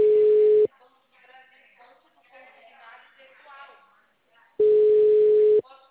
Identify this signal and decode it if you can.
Ringback tone (Europe, 425 Hz), with speech.